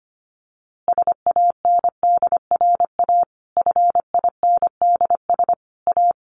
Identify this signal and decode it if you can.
Morse code, heard as a keyed tone, about 25 wpm, 700 Hz. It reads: SANDRA FINDS A
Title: Easy Phrases 25wpm